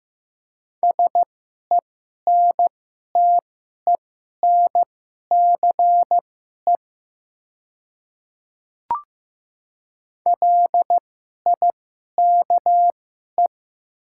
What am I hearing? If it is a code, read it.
Morse code, heard as a keyed tone, about 15 wpm, spaced as for 11 wpm, 700 Hz. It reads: SENTENCE LIKE
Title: 2x Character Spacing - Top 200 Words - 15wpm